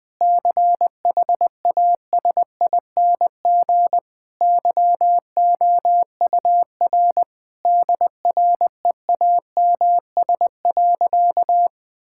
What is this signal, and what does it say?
Morse code, heard as a keyed tone, about 20 wpm, 700 Hz. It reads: CHASING YOUR DREAMS.
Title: Difficult Sentences with Hints - 20wpm